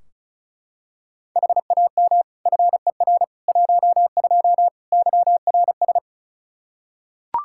Morse code, heard as a keyed tone, about 34 wpm, 700 Hz. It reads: HAM FER 12 YRS
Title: Practice QSO - Exchange 3 - More Info - 35wpm